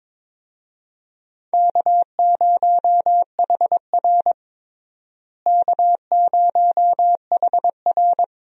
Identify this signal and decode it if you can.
Morse code, heard as a keyed tone, about 22 wpm, 700 Hz. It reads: K0HR K0HR
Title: USA Call Signs - 4 Characters - Repeated 3x - 22wpm